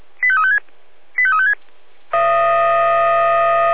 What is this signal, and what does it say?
Five-tone selective call (ZVEI), heard as quick runs of five tones, tones of about 70 ms. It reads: 85356 85356